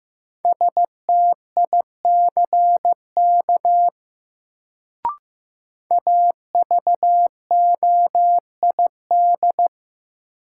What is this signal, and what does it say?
Morse code, heard as a keyed tone, about 15 wpm, 700 Hz. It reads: STICK AVOID
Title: Sentences from Top 1200 Words - New Word Drill Alt - 15wpm